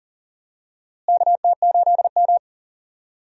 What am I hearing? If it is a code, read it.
Morse code, heard as a keyed tone, about 39 wpm, 700 Hz. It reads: KT8M
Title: USA Call Signs - 4 Characters - Repeated 3x - 40wpm